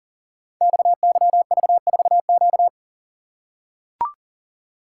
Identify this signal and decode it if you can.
Morse code, heard as a keyed tone, about 39 wpm, 700 Hz. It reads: XYV4Q
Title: Code Groups - 5 Letter:Numbers - 40wpm